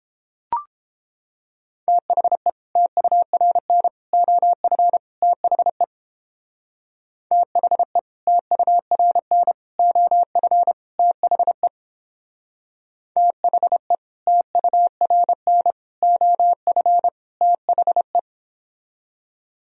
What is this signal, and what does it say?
Morse code, heard as a keyed tone, about 31 wpm, 700 Hz. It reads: THE TURN OF THE THE TURN OF THE THE TURN OF THE